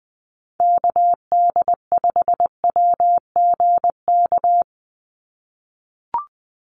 Morse code, heard as a keyed tone, about 20 wpm, 700 Hz. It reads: KD5WGK
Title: USA Call Signs 20wpm